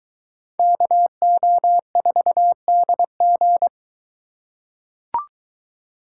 Morse code, heard as a keyed tone, about 22 wpm, 700 Hz. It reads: KO4DG